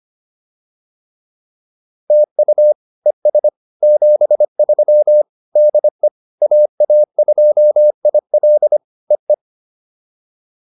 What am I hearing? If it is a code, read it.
Morse code, heard as a keyed tone, about 25 wpm, 560 Hz. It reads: TU ES 73 DE AA2IL EE